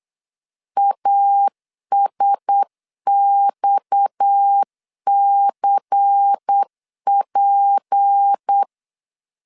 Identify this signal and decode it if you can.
Morse code, heard as a keyed tone, about 8 wpm, 800 Hz. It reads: ASXCP
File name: LEZIONI / 7 GF9Z / 7 GF9Z 8wpm 3213